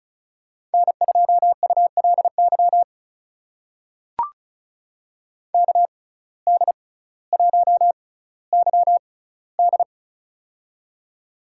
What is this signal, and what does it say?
Morse code, heard as a keyed tone, about 36 wpm, spaced as for 18 wpm, 700 Hz. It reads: N2ULY KD1YD
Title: USA Call Signs - 5 Characters - 6x Character Spacing - 35wpm